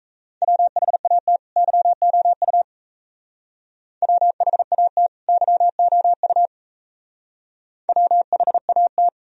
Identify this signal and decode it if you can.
Morse code, heard as a keyed tone, about 33 wpm, 700 Hz. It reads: WHAT YOU WHAT YOU WHAT
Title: Speed-Racing - Sets of 2 Words 28wpm